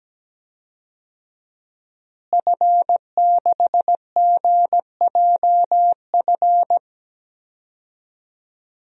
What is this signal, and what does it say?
Morse code, heard as a keyed tone, about 17 wpm, 700 Hz. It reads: F6GJF
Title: DXCC-100 Call Signs - 5 Characters - Repeated 3x - 17wpm